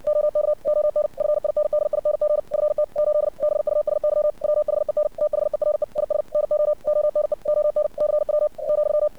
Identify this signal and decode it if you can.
Morse code, heard as a keyed tone, about 60 wpm, 600 Hz. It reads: OK ON FINDING FT 8 BUSY WHEN THERE IS NO ONE ON CW ,
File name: archived / CRSnet